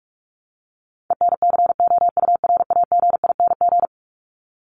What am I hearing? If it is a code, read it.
Morse code, heard as a keyed tone, about 44 wpm, 700 Hz. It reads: ENCOURAGING
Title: Top 1751-2000 Words 45wpm